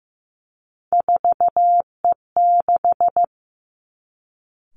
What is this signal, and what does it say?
Morse code, heard as a keyed tone, about 15 wpm, 700 Hz. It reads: 4E6
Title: Three Challenging Letter-Number Combinations 15wpm